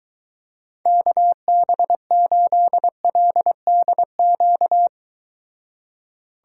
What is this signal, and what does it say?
Morse code, heard as a keyed tone, about 22 wpm, 700 Hz. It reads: KB8LDQ